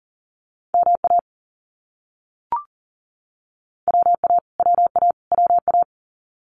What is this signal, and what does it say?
Morse code, heard as a keyed tone, about 40 wpm, 700 Hz. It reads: MA WA WA WA
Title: US States Repeated 3x 40wpm